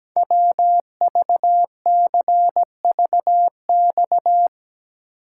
Morse code, heard as a keyed tone, about 17 wpm, 700 Hz. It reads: WVCVX
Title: Code Groups - 5 Letter:Numbers - 17wpm